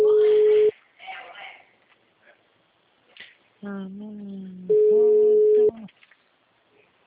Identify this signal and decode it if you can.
Ringback tone (Europe, 425 Hz), with speech.
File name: out- -2008- - - .6